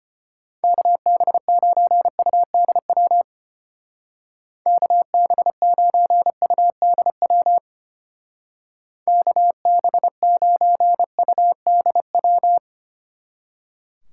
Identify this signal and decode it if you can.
Morse code, heard as a keyed tone, about 29 wpm, 700 Hz. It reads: KB9UDW KB9UDW KB9UDW
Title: Speed Racing - USA 2x3 Call Signs 25wpm